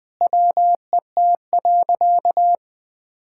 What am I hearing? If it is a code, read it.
Morse code, heard as a keyed tone, about 20 wpm, 700 Hz. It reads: WET.
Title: Sentences with 5-Word Phrases - 20wpm